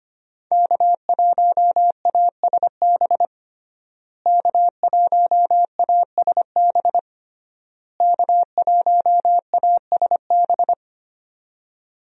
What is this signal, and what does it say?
Morse code, heard as a keyed tone, about 25 wpm, 700 Hz. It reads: K1ASB K1ASB K1ASB
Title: USA Call Signs - 5 Characters - Repeated 3x - 25wpm